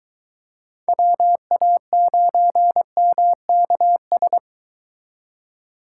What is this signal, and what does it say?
Morse code, heard as a keyed tone, about 24 wpm, 700 Hz. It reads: WA9MKS